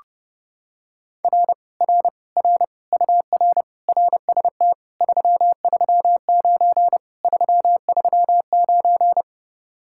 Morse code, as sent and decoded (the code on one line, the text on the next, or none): .-. / .-. / .-. / ..- .-. / .-. ... - / ...-- ...-- ----. / ...-- ...-- ----.
R R R UR RST 339 339